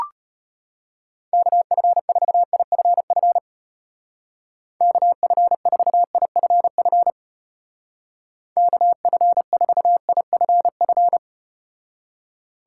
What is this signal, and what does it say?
Morse code, heard as a keyed tone, about 33 wpm, 700 Hz. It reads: KF4IFF KF4IFF KF4IFF